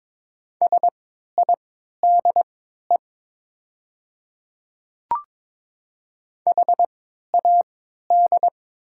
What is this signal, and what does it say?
Morse code, heard as a keyed tone, about 23 wpm, spaced as for 13 wpm, 700 Hz. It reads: SIDE HAD
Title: 3x Character Spacing - Top 200 Words - 22wpm